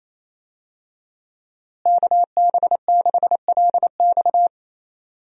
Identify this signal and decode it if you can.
Morse code, heard as a keyed tone, about 28 wpm, 700 Hz. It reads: KB6LX
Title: USA Call Signs - 5 Characters - Repeated 2x - 28wpm